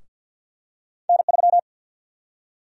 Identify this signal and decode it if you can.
Morse code, heard as a keyed tone, about 52 wpm, 700 Hz. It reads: N3
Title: USA Call Sign Prefixes - Repeated 2x - 50wpm